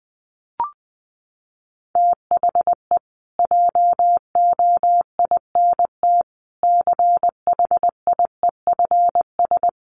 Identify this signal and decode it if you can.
Morse code, heard as a keyed tone, about 20 wpm, 700 Hz. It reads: THE JOINT CHIEFS
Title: Sets of 5 Words Encore2 20wpm